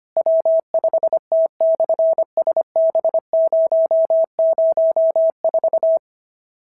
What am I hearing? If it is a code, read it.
Morse code, heard as a keyed tone, about 25 wpm, 650 Hz. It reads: W5T/SB004
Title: SOTA QSO Practice 25wpm